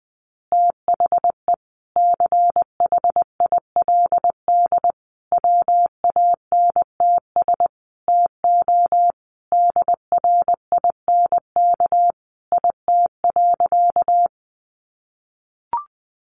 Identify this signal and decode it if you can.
Morse code, heard as a keyed tone, about 20 wpm, 700 Hz. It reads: THE CHILD WANTS TO DRINK IT.
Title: Additional Easy Phrases 20wpm